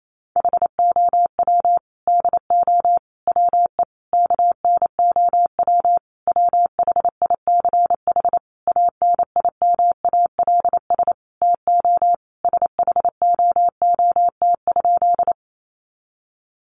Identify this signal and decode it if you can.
Morse code, heard as a keyed tone, about 28 wpm, 700 Hz. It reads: HOW DO WE KNOW WHICH ANIMALS TO SHOOT?
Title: A Sound Of Thunder 28wpm